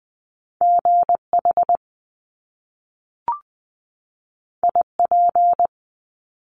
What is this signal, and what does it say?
Morse code, heard as a keyed tone, about 20 wpm, 700 Hz. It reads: GH IP
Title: Top Two Letter Combinations in Words 20wpm